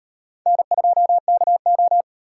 Morse code, heard as a keyed tone, about 38 wpm, 700 Hz. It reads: N2KO